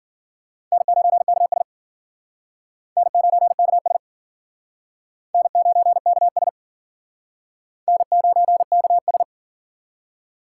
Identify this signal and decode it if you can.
Morse code, heard as a keyed tone, about 50 wpm, 700 Hz. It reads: N9KS N9KS N9KS N9KS